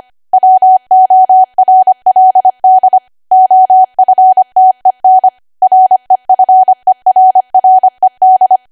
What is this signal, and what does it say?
Morse code, heard as a keyed tone, about 25 wpm, 750 Hz. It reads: WORLD OFTEN REFERRED
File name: Archive / 25 WPM / 25WPM